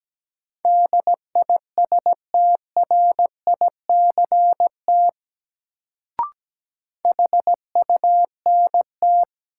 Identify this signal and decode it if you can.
Morse code, heard as a keyed tone, about 17 wpm, 700 Hz. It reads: DISTRICT HUNT
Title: Sentences from Top 1600 Words - New Word Drill Alt - 17wpm